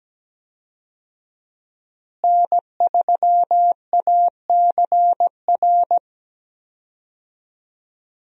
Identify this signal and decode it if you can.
Morse code, heard as a keyed tone, about 17 wpm, 700 Hz. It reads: N3ACR
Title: USA Call Signs - 5 Characters - Repeated 3x - 17wpm